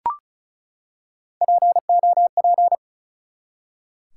Morse code, heard as a keyed tone, about 36 wpm, 700 Hz. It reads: POP
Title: Three Letter Words - 35wpm